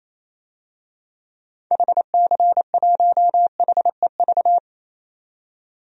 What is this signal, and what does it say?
Morse code, heard as a keyed tone, about 28 wpm, 700 Hz. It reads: HC1HEV
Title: DXCC-100 Call Signs - 6 Characters - 28wpm